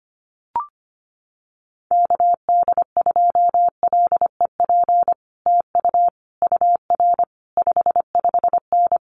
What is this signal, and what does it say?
Morse code, heard as a keyed tone, about 25 wpm, 700 Hz. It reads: KD2LEP TU UR 55N
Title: Common QSO Phrases Encore 25wpm